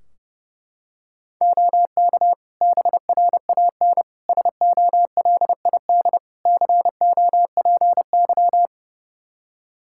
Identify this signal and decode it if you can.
Morse code, heard as a keyed tone, about 30 wpm, 700 Hz. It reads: OK BRAN SOLID COPY
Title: Practice QSO - Exchange 3 - More Info - 30wpm